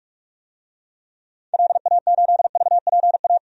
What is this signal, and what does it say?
Morse code, heard as a keyed tone, about 47 wpm, 700 Hz. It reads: LA8VPA
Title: DXCC-100 Call Signs - 6 Characters - 45wpm